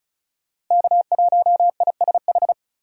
Morse code, heard as a keyed tone, about 36 wpm, 700 Hz. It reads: K1ISH
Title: USA Call Signs - 5 Characters - Repeated 3x - 35wpm